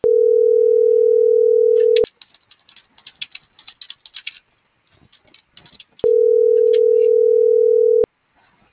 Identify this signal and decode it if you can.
Ringback tone (North America), 440 + 480 Hz.